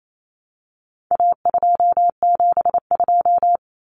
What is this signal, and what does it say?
Morse code, heard as a keyed tone, about 28 wpm, 700 Hz. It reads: A272
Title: Common Cut Numbers - 4-Digit 28wpm